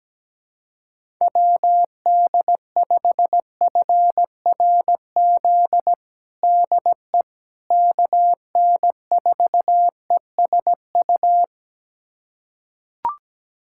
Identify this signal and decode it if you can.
Morse code, heard as a keyed tone, about 17 wpm, 700 Hz. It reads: WD5FRZ DE KN4ESU
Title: USA Callsign DE Callsign 17wpm